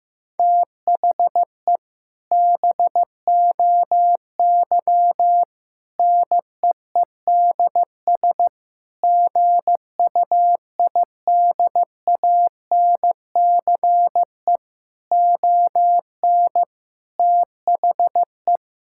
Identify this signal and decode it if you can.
Morse code, heard as a keyed tone, about 16 wpm, 700 Hz. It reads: THE BOY NEEDS GUIDANCE ON THE
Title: Sentences from Top 1600 Words 15wpm